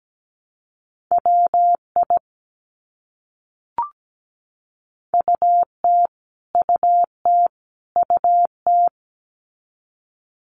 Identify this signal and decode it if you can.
Morse code, heard as a keyed tone, about 17 wpm, 700 Hz. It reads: WI UT UT UT
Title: US States Repeated 3x 17wpm